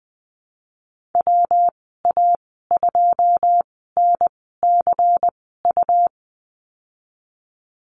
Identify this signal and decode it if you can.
Morse code, heard as a keyed tone, about 20 wpm, spaced as for 15 wpm, 700 Hz. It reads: WA2NCU
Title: 167 Lesson 27g - Number 7 - Call Signs 20wpm